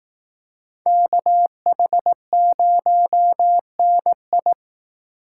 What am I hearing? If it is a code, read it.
Morse code, heard as a keyed tone, about 19 wpm, 700 Hz. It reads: KH0NI
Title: Speed Racing - USA 2x2 Call Signs 15wpm